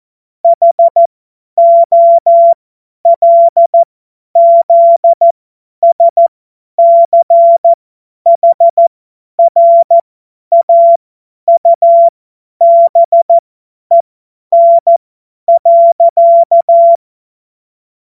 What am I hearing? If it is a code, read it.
Morse code, heard as a keyed tone, about 14 wpm, spaced as for 10 wpm, 680 Hz. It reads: HOLZSCHRAUBEN.
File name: Morsekurs / Witze / Witz 17.Oktober 2023 50 F70